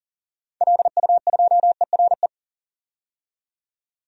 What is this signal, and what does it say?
Morse code, heard as a keyed tone, about 40 wpm, 700 Hz. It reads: LU2ERE
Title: DXCC-100 Call Signs - 6 Characters - 40wpm